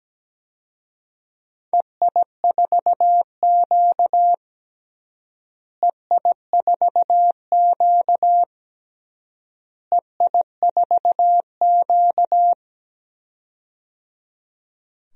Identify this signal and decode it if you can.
Morse code, heard as a keyed tone, about 17 wpm, 700 Hz. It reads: EI4Q EI4Q EI4Q
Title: DXCC-100 Call Signs - 4 Characters - Repeated 3x - 17wpm